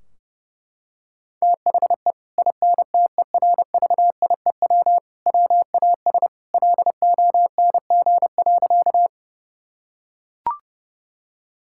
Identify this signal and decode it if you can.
Morse code, heard as a keyed tone, about 32 wpm, 700 Hz. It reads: THE INTERVIEW WAS LONG.
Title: Sentences from Top 1600 Words 30wpm